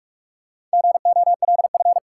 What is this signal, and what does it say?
Morse code, heard as a keyed tone, about 46 wpm, 700 Hz. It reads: GOLF
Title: Four Letter Words - 45wpm